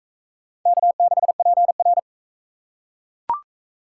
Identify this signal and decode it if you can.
Morse code, heard as a keyed tone, about 37 wpm, 700 Hz. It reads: K6PR